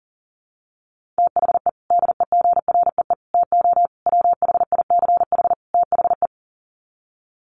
Morse code, heard as a keyed tone, about 40 wpm, 700 Hz. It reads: THE DEGREE TO WHICH THE
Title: Sets of 5 Words Encore2 40wpm